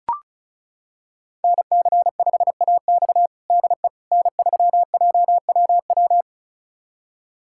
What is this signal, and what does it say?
Morse code, heard as a keyed tone, about 36 wpm, 700 Hz. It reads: NC5AX DE N3JWW
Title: USA Callsign DE Callsign - 5 Characters - 35wpm